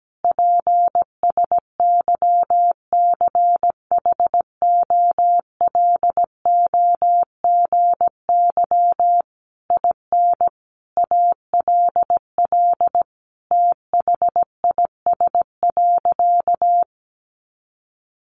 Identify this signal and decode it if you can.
Morse code, heard as a keyed tone, about 17 wpm, 700 Hz. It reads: PSYCHOLOGY IN ALL THIS.
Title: Sentences from Top 1400 Words 17wpm